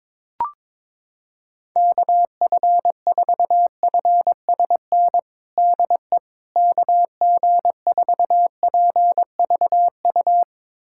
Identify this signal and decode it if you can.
Morse code, heard as a keyed tone, about 22 wpm, 700 Hz. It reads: KF4FSN DE KG4PVU
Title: USA Callsign DE Callsign - 6 Characters - 22wpm